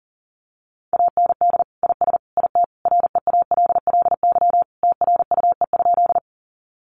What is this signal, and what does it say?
Morse code, heard as a keyed tone, about 40 wpm, 700 Hz. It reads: AND IS IT REALLY TRUE?
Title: Sentences from Top 600 Words 40wpm